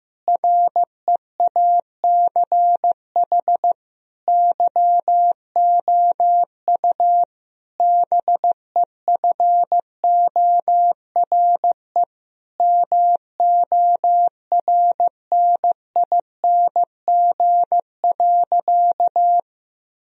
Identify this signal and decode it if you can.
Morse code, heard as a keyed tone, about 15 wpm, 700 Hz. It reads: REACH YOU BEFORE MORNING.
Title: Sentences from Top 1000 Words - Review Part 2 - 15wpm